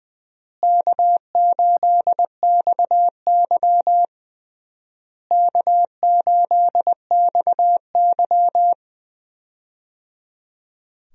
Morse code, heard as a keyed tone, about 20 wpm, 700 Hz. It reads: K8XY K8XY
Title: USA Call Signs - 4 Characters - Repeated 3x - 20wpm